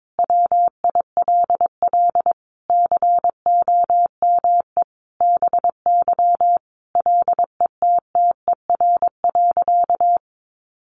Morse code, heard as a keyed tone, about 22 wpm, 700 Hz. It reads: WILL COME BY LETTER.
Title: Sentences from Top 1400 Words 22wpm